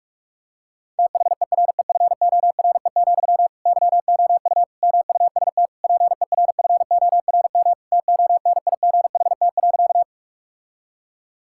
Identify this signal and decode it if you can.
Morse code, heard as a keyed tone, about 46 wpm, 700 Hz. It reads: THEREFORE, YOU MUST PERFORM TONIGHT.
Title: Sentences from Top 1000 Words - Review Part 5 - 45wpm